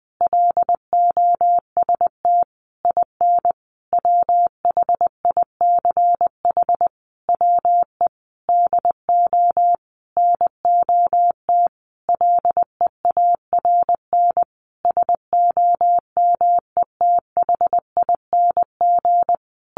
Morse code, heard as a keyed tone, about 20 wpm, 700 Hz. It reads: LOST IN WHICH WE DO NOT LEARN SOMETHING
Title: Sentences from Top 1400 Words 20wpm